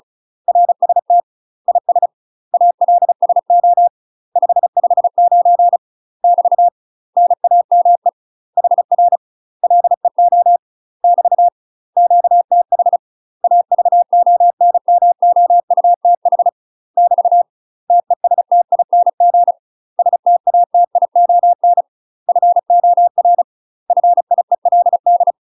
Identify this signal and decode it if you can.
Morse code, heard as a keyed tone, about 36 wpm, 700 Hz. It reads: RST IS ALSO 559 <BT> NAME HR LEO <BT> QTH AVONMOUTH <BT> TESTING STATION FOR FIELD
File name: UK MASTER QSOs TRAINING / 35WPM / UK MASTER QSOs 0010